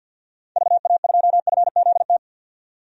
Morse code, heard as a keyed tone, about 50 wpm, 700 Hz. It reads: VA2FZT
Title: DXCC-100 Call Signs - 6 Characters - 50wpm